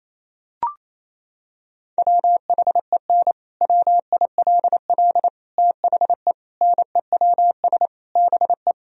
Morse code, heard as a keyed tone, about 28 wpm, 700 Hz. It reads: WHEN WILL THE NEWS BE
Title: Sentences from Top 1000 Words - Review Part 3 - 28wpm